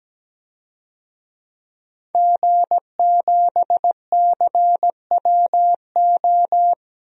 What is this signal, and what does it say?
Morse code, heard as a keyed tone, about 17 wpm, 700 Hz. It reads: G7CWO
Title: DXCC-100 Call Signs - 5 Characters - Repeated 2x - 17wpm